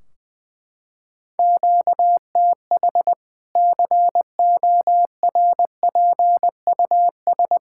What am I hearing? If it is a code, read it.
Morse code, heard as a keyed tone, about 20 wpm, 700 Hz. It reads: QTH CORPUS
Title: Practice QSO - Exchange 1 - Beginning - 20wpm